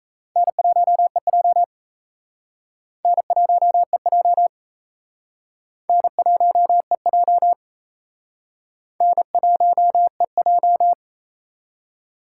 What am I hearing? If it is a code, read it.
Morse code, heard as a keyed tone, about 35 wpm, 700 Hz. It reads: N1EJ N1EJ N1EJ N1EJ
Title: Speed Racing - USA 1x2 Call Signs 28wpm